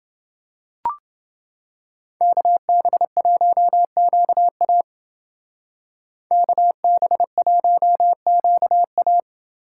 Morse code, heard as a keyed tone, about 29 wpm, 700 Hz. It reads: KB1QA KB1QA